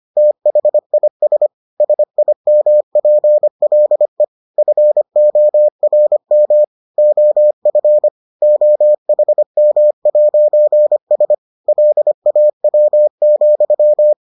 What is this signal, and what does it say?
Morse code, heard as a keyed tone, about 25 wpm, 600 Hz. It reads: THIS SIMPLE FORM OF OHM'S LAW,